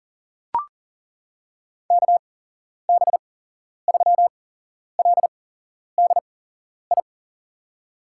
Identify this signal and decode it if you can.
Morse code, heard as a keyed tone, about 40 wpm, spaced as for 11 wpm, 700 Hz. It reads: KB3LDI
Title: USA Call Signs - 8x Character Spacing - 40wpm